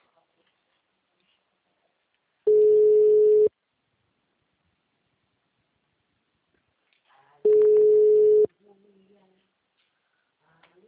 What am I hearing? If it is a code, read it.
Ringback tone (Europe, 425 Hz), with speech.
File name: out- -2034- - - .4954